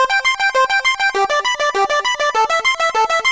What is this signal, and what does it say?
There is no code.